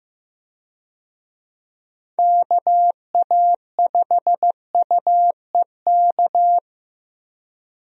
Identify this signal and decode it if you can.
Morse code, heard as a keyed tone, about 15 wpm, 700 Hz. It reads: KA5UEK
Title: USA Call Signs - 6 Characters - Repeated 3x - 15wpm